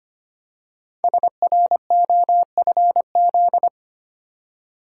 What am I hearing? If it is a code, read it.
Morse code, heard as a keyed tone, about 25 wpm, 700 Hz. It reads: SROFZ
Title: Code Groups - 5 Letters - 25wpm